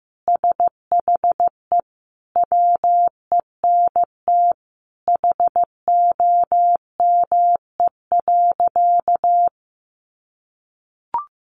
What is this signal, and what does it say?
Morse code, heard as a keyed tone, about 15 wpm, 700 Hz. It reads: SHE WENT HOME.
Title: Sentences from Top 800 Words 15wpm